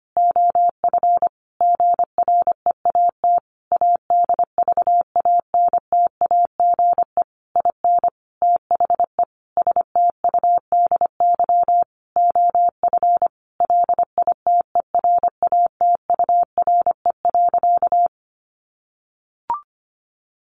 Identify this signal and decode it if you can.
Morse code, heard as a keyed tone, about 25 wpm, 700 Hz. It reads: OF GREAT ADVANTAGE IN THE STUDY OF LITERATURE.
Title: Sentences from Top 900 Words 25wpm